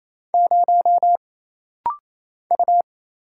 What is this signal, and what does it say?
Morse code, heard as a keyed tone, about 27 wpm, 700 Hz. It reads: 0 U
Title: Single Letter-Number - Rapid-Fire 28wpm